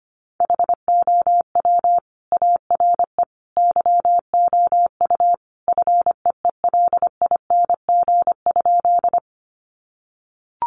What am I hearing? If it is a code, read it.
Morse code, heard as a keyed tone, about 25 wpm, 700 Hz. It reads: HOW ARE YOU FEELING?
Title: Sentences from Top 1200 Words 25wpm